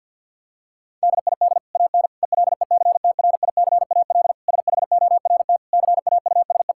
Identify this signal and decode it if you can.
Morse code, heard as a keyed tone, about 48 wpm, 700 Hz. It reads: DID AN ELECTRICAL SHORT CAUSE
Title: Sentences from Top 1000 Words - Review Part 6 - 50wpm